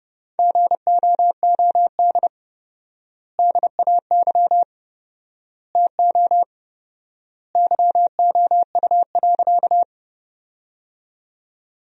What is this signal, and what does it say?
Morse code, heard as a keyed tone, about 30 wpm, 700 Hz. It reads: GOOD DAY TO YOU.
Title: 4x Word-Spacing - Sentences from Top 200 Words - 30wpm